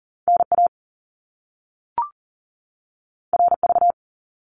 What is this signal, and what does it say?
Morse code, heard as a keyed tone, about 39 wpm, 700 Hz. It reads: NA RV
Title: Top Two Letter Combinations in Words 40wpm